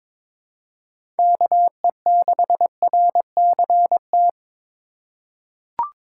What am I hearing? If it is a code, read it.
Morse code, heard as a keyed tone, about 22 wpm, 700 Hz. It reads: KE6RCT
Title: USA Call Signs - 6 Characters - 2x Character Spacing - 22wpm